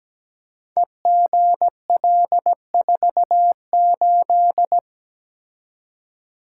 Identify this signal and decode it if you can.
Morse code, heard as a keyed tone, about 17 wpm, 700 Hz. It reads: EGL48
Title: Code Groups - 5 Letter:Numbers - 17wpm